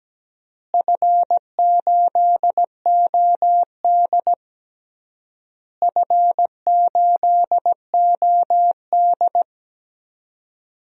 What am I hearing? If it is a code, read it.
Morse code, heard as a keyed tone, about 17 wpm, 700 Hz. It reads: F8OD F8OD
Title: DXCC-100 Call Signs - 4 Characters - Repeated 3x - 17wpm